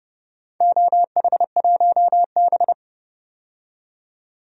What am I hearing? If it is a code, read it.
Morse code, heard as a keyed tone, about 30 wpm, 700 Hz. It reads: OH1B
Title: DXCC-100 Call Signs - 4 Characters - 30wpm